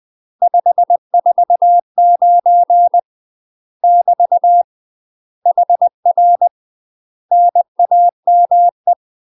Morse code, heard as a keyed tone, about 20 wpm, 700 Hz. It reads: 549 <BT> HR NAME
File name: UK MASTER QSOs TRAINING / 20WPM / UK MASTER QSOs 0069